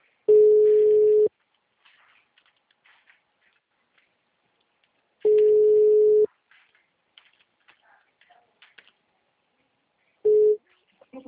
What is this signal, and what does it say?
Ringback tone (Europe, 425 Hz).
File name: out- -2037- - - .22687